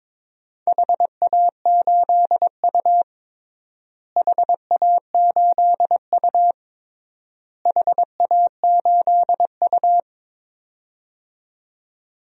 Morse code, heard as a keyed tone, about 22 wpm, 700 Hz. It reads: HA8U HA8U HA8U
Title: DXCC-100 Call Signs - 4 Characters - Repeated 3x - 22wpm